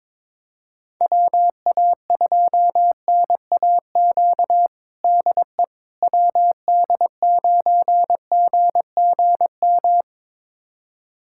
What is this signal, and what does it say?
Morse code, heard as a keyed tone, about 22 wpm, 700 Hz. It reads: WA2NAQ DE WD9GGM
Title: USA Callsign DE Callsign - 6 Characters - 22wpm